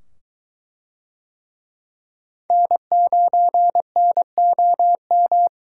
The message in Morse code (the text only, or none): N9NOM